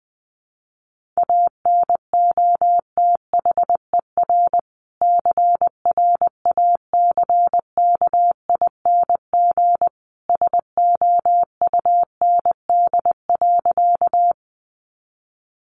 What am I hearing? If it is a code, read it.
Morse code, heard as a keyed tone, about 20 wpm, 700 Hz. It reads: ANOTHER CRACKING SOUND.
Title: A Sound Of Thunder 20wpm